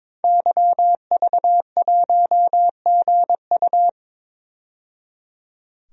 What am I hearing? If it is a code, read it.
Morse code, heard as a keyed tone, about 22 wpm, 700 Hz. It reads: YV1GU
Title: DXCC-100 Call Signs - 5 Characters - 22wpm